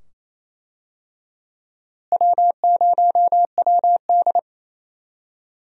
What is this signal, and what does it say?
Morse code, heard as a keyed tone, about 27 wpm, 700 Hz. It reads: W0WD
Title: USA Call Signs - 4 Characters - Repeated 2x - 28wpm